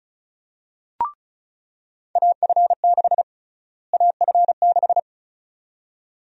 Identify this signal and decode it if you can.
Morse code, heard as a keyed tone, about 36 wpm, 700 Hz. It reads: AF6 AF6
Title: USA Call Sign Prefixes - Repeated 2x - 35wpm